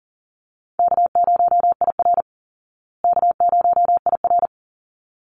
Morse code, heard as a keyed tone, about 40 wpm, 700 Hz. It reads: K0IR K0IR
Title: Easy Call Signs Repeated 2x - 40wpm